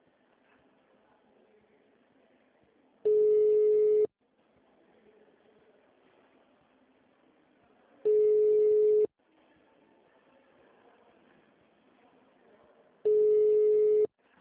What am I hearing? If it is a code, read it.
Ringback tone (Europe, 425 Hz), with speech.